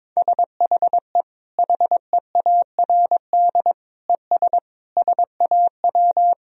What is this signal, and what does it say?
Morse code, heard as a keyed tone, about 22 wpm, 700 Hz. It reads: SHE HEARD ES SAW
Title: Sentences from Top 500 Words and Abbreviations 22wpm